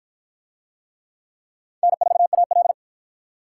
Morse code, heard as a keyed tone, about 50 wpm, 700 Hz. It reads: N4AL